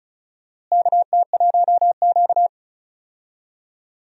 Morse code, heard as a keyed tone, about 36 wpm, 700 Hz. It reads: KT1Q
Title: USA Call Signs - 4 Characters - Repeated 3x - 35wpm